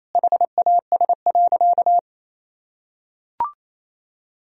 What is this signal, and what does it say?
Morse code, heard as a keyed tone, about 27 wpm, 700 Hz. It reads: HAS.
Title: Sentences from Top 1000 Words - Review Part 2 - 28wpm